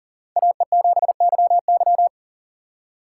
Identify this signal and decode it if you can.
Morse code, heard as a keyed tone, about 40 wpm, 700 Hz. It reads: AE7YY
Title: USA Call Signs - 6x Character Spacing - 40wpm